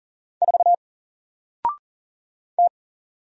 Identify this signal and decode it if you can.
Morse code, heard as a keyed tone, about 40 wpm, 700 Hz. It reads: 4 T
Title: Single Letter-Number Alt 40wpm